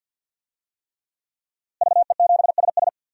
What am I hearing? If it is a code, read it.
Morse code, heard as a keyed tone, about 51 wpm, 700 Hz. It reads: VE7SS
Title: DXCC-100 Call Signs - 5 Characters - Repeated 3x - 50wpm